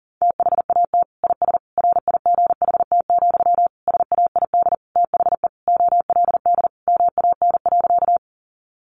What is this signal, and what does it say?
Morse code, heard as a keyed tone, about 40 wpm, 700 Hz. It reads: THAT IS RIGHT, SAID THE OLD MAN.
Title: Sentences from Top 200 Words 40wpm